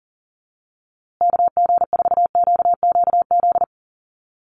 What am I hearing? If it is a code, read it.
Morse code, heard as a keyed tone, about 40 wpm, 700 Hz. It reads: KG4QQZ
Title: USA Call Signs Repeated 2x 40wpm